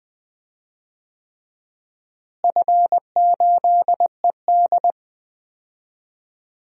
Morse code, heard as a keyed tone, about 19 wpm, 700 Hz. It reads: F8ED
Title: DXCC-100 Call Signs - 4 Characters - Repeated 3x - 20wpm